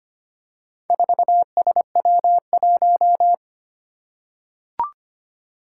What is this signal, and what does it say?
Morse code, heard as a keyed tone, about 25 wpm, 700 Hz. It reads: 4SW1
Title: Four Challenging Letter-Number Combinations 25wpm